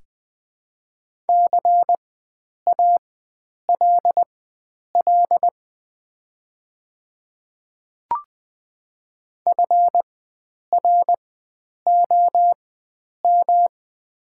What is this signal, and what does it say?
Morse code, heard as a keyed tone, about 20 wpm, spaced as for 9 wpm, 700 Hz. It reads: CALL FROM
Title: 4x Character Spacing - Top 100 Words - 20wpm